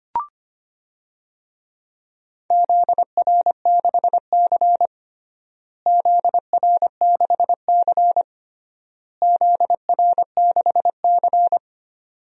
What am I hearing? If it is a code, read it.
Morse code, heard as a keyed tone, about 25 wpm, 700 Hz. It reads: ZR6C ZR6C ZR6C
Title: DXCC-100 Call Signs - 4 Characters - Repeated 3x - 25wpm